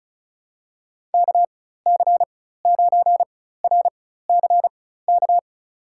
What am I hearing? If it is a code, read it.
Morse code, heard as a keyed tone, about 36 wpm, spaced as for 16 wpm, 700 Hz. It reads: KC9RCK
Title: USA Call Signs - 6 Characters - 4x Character Spacing - 35wpm